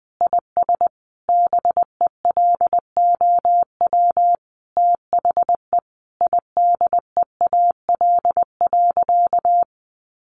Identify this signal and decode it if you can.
Morse code, heard as a keyed tone, about 20 wpm, 700 Hz. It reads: IS BELOW THE IDEAL.
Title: Sentences from Top 1400 Words 20wpm